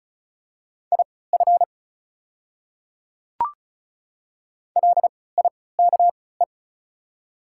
Morse code, heard as a keyed tone, about 34 wpm, spaced as for 20 wpm, 700 Hz. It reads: IF LIKE
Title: 3x Character Spacing - Top 200 Words - 35wpm